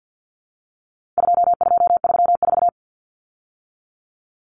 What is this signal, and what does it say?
Morse code, heard as a keyed tone, about 50 wpm, 700 Hz. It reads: 2234
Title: Common Cut Numbers - 4-Digit 50wpm